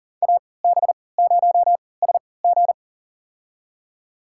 Morse code, heard as a keyed tone, about 40 wpm, spaced as for 23 wpm, 700 Hz. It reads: AB0SG
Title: USA Call Signs - 5 Characters - 3x Character Spacing - 40wpm